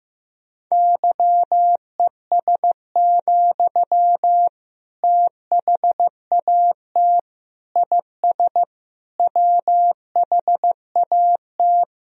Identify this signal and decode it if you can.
Morse code, heard as a keyed tone, about 15 wpm, 700 Hz. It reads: YES, THAT IS WHAT
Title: Sentences from Top 500 Words and Abbreviations 15wpm